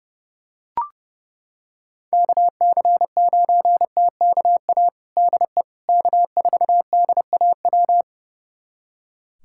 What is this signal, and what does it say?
Morse code, heard as a keyed tone, about 30 wpm, 700 Hz. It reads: KC9TKA DE K4DAW
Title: USA Callsign DE Callsign 30wpm